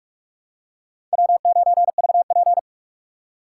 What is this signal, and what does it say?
Morse code, heard as a keyed tone, about 44 wpm, 700 Hz. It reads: W9VP